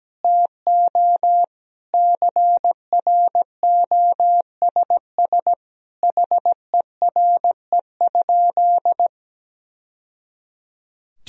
Morse code, heard as a keyed tone, about 17 wpm, 700 Hz. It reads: TO CROSS HERE?
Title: Sentences from Top 1000 Words - Review Part 1 - 17wpm